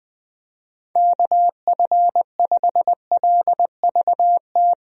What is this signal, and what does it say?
Morse code, heard as a keyed tone, about 20 wpm, 700 Hz. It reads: KF5LVT
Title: USA Call Signs - 6 Characters - Repeated 3x - 20wpm